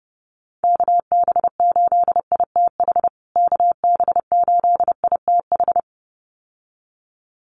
Morse code, heard as a keyed tone, about 30 wpm, 700 Hz. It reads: KB8ITH KB8ITH
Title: USA Call Signs Repeated 2x 30wpm